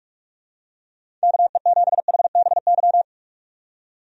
Morse code, heard as a keyed tone, about 46 wpm, 700 Hz. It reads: KE7HBY
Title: USA Call Signs - 6 Characters - 45wpm